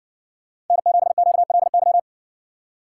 Morse code, heard as a keyed tone, about 60 wpm, 700 Hz. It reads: N7QLY